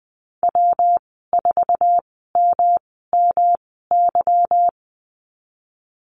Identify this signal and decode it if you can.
Morse code, heard as a keyed tone, about 20 wpm, spaced as for 15 wpm, 700 Hz. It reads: W4MMY
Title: 161 Lesson 26g - Letter g - Call Signs 20wpm